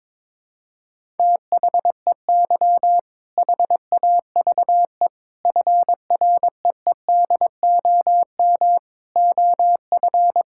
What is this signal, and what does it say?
Morse code, heard as a keyed tone, about 22 wpm, 700 Hz. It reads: THEY HAVE FREEDOM OF
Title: Sentences from Top 1000 Words - Review Part 5 - 22wpm